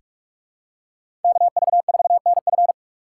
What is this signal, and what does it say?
Morse code, heard as a keyed tone, about 46 wpm, 700 Hz. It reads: KV4NF
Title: USA Call Signs - 6x Character Spacing - 45wpm